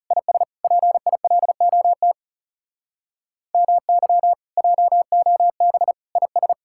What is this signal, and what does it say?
Morse code, heard as a keyed tone, about 38 wpm, 700 Hz. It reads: IS PILOT MY JOB IS